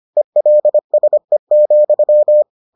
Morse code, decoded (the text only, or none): ELSE,